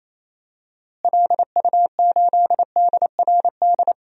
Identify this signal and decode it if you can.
Morse code, heard as a keyed tone, about 28 wpm, 700 Hz. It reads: LU8DRD
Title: DXCC-100 Call Signs - 6 Characters - Repeated 3x - 28wpm